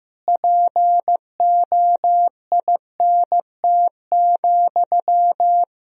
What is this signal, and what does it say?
Morse code, heard as a keyed tone, about 16 wpm, 700 Hz. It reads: POINT,
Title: Sentences from Top 1000 Words - Review Part 4 - 15wpm